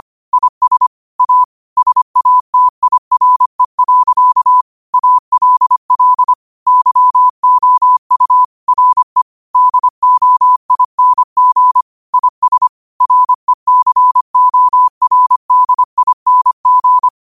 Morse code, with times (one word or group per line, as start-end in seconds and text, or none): 0.3-0.9 IS
1.2-1.4 A
1.8-4.6 SATIRE.
4.9-6.3 ALL
6.7-8.4 YOU
8.7-9.2 RE
9.6-11.8 DOING
12.1-12.7 IS
13.0-17.1 RECORDING